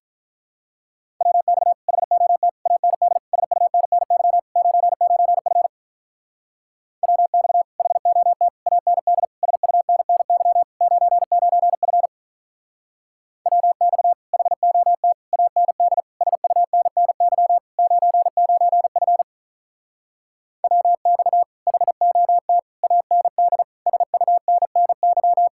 Morse code, heard as a keyed tone, about 44 wpm, 700 Hz. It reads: WX HOT AND SUNNY 99F WX HOT AND SUNNY 99F WX HOT AND SUNNY 99F WX HOT AND SUNNY
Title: Speed-Racing Common QSO Phrases Encore - 35wpm